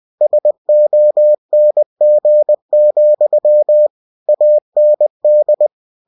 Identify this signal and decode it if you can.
Morse code, heard as a keyed tone, about 20 wpm, 600 Hz. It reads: SONG, AND
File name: winnie-20wpm